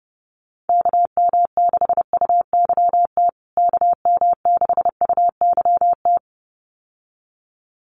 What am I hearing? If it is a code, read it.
Morse code, heard as a keyed tone, about 30 wpm, 700 Hz. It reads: KM6UYT KM6UYT
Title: USA Call Signs Repeated 2x 30wpm